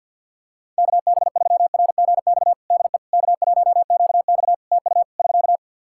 Morse code, heard as a keyed tone, about 53 wpm, 700 Hz. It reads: KB3RGX DE K1QX TU <SK>
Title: Practice QSO - Exchange 4 - Ending - 50wpm